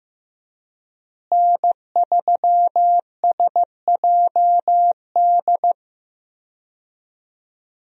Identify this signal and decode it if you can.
Morse code, heard as a keyed tone, about 16 wpm, 700 Hz. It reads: N3SJD
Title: USA Call Signs - 5 Characters - Repeated 3x - 15wpm